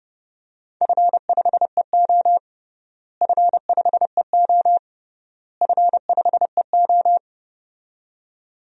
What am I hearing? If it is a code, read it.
Morse code, heard as a keyed tone, about 30 wpm, 700 Hz. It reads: F5EO F5EO F5EO
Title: DXCC-100 Call Signs - 4 Characters - Repeated 3x - 30wpm